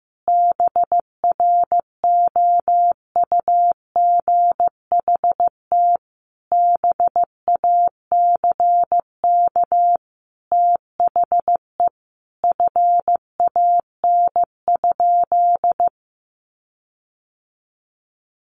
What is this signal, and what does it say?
Morse code, heard as a keyed tone, about 15 wpm, 700 Hz. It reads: BROUGHT BACK THE FAN?
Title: Sentences from Top 1300 Words 15wpm